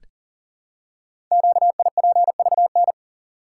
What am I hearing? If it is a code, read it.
Morse code, heard as a keyed tone, about 40 wpm, 700 Hz. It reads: QIPVN
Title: Code Groups - 5 Letter:Numbers - 40wpm